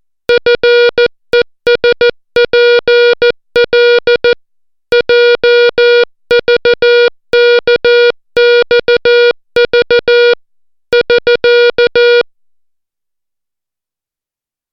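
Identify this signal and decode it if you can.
Morse code, heard as a keyed tone, about 14 wpm, 480 Hz. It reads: FESPL JVKXV <SK>
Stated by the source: Morse alphabet lesson, 60 characters per minute.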